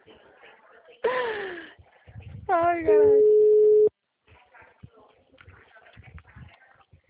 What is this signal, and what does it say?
Ringback tone (Europe, 425 Hz), with speech.